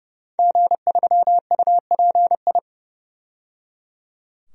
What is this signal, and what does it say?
Morse code, heard as a keyed tone, about 30 wpm, 700 Hz. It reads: G3UPI
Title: DXCC-100 Call Signs - 5 Characters - 30wpm